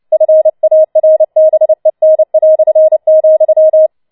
Morse code, heard as a keyed tone, about 29 wpm, 630 Hz. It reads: FARBEN",
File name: Schneewittchen Teil3 30wpm